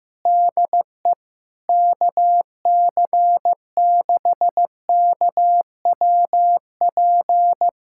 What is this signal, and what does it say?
Morse code, heard as a keyed tone, about 15 wpm, 700 Hz. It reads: DE KC6KWP
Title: USA Callsign DE Callsign - 6 Characters - 15wpm